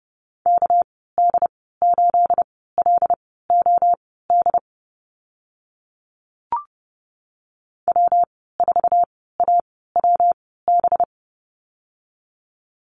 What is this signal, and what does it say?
Morse code, heard as a keyed tone, about 30 wpm, spaced as for 17 wpm, 700 Hz. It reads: KD8LOD W4AWB
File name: Lesson 37g - Letter x - Call Signs 30wpm